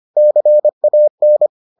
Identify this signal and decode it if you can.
Morse code, heard as a keyed tone, about 25 wpm, 600 Hz. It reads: CAN